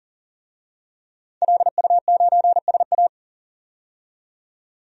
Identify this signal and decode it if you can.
Morse code, heard as a keyed tone, about 40 wpm, 700 Hz. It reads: LU9SA
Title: DXCC-100 Call Signs - 5 Characters - 40wpm